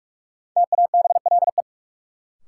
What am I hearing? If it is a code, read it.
Morse code, heard as a keyed tone, about 46 wpm, 700 Hz. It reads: TABLE
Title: Sentences from Top 400 Words - New Word Drill - 45wpm